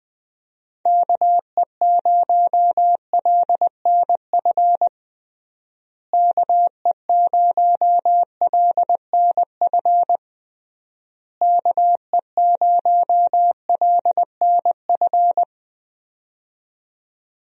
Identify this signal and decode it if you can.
Morse code, heard as a keyed tone, about 20 wpm, 700 Hz. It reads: KE0LNF KE0LNF KE0LNF
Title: USA Call Signs - 6 Characters - Repeated 3x - 20wpm